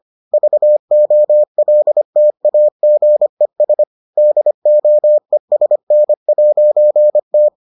Morse code, heard as a keyed tone, about 25 wpm, 600 Hz. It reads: VOLTAGES DOESN'T